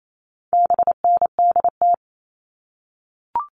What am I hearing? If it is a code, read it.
Morse code, heard as a keyed tone, about 28 wpm, 700 Hz. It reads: BNDT
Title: All Cut Numbers - 4-Digit 28wpm